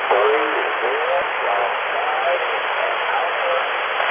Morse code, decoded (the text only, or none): none